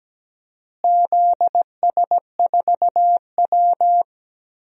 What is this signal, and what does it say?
Morse code, heard as a keyed tone, about 17 wpm, 700 Hz. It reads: ZS4W
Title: DXCC-100 Call Signs - 4 Characters - Repeated 2x - 17wpm